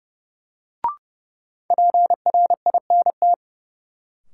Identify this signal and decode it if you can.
Morse code, heard as a keyed tone, about 32 wpm, 700 Hz. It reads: PRINT
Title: Sentences from Top 1400 Words - New Word Drill Alt - 30wpm